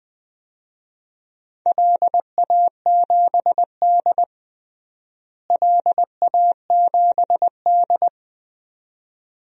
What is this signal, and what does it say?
Morse code, heard as a keyed tone, about 20 wpm, 700 Hz. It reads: LA7D LA7D
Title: DXCC-100 Call Signs - 4 Characters - Repeated 2x - 20wpm